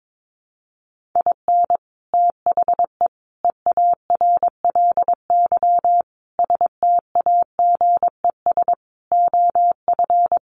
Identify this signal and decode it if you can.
Morse code, heard as a keyed tone, about 22 wpm, 700 Hz. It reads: IN THE EARLY STAGES OF
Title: Sets of 5 Words Encore 22wpm